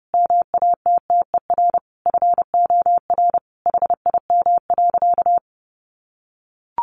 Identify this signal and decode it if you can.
Morse code, heard as a keyed tone, about 30 wpm, 700 Hz. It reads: MATTER FOR HIM.
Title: Sentences from Top 1100 Words 30wpm